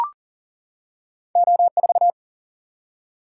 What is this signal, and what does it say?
Morse code, heard as a keyed tone, about 40 wpm, 700 Hz. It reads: O4
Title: Single-Letter Single-Number 40wpm